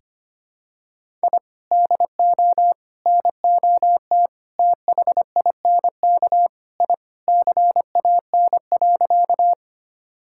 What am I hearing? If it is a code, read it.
Morse code, heard as a keyed tone, about 25 wpm, 700 Hz. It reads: I DO NOT THINK I CAN.
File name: practice / 25wpm / Sentences with 4-Word Phrases - 25wpm